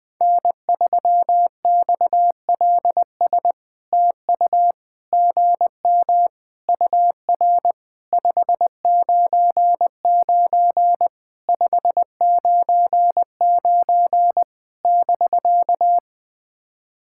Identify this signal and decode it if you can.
Morse code, heard as a keyed tone, about 20 wpm, 700 Hz. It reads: N3XLS TU GM UR 599 599 <BK>
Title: POTA QSO Practice - 20wpm